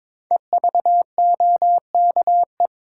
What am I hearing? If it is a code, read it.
Morse code, heard as a keyed tone, about 22 wpm, 700 Hz. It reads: EVOKE
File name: 5 Letter Words - First 3 Letters Given - 22wpm